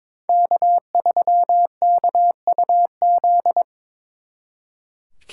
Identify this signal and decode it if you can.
Morse code, heard as a keyed tone, about 25 wpm, 700 Hz. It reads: K3KUZ
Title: Speed Racing - USA 1x3 Call Signs 22wpm